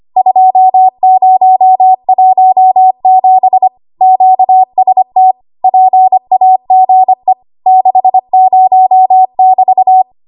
Morse code, heard as a keyed tone, about 25 wpm, 750 Hz. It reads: 2017 QST PAGE 60<BT>
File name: Archive / 25 WPM / 25WPM